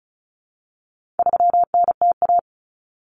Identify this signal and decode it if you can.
Morse code, heard as a keyed tone, about 34 wpm, 700 Hz. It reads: 3NTA
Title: Common Cut Numbers - 4-Digit 35wpm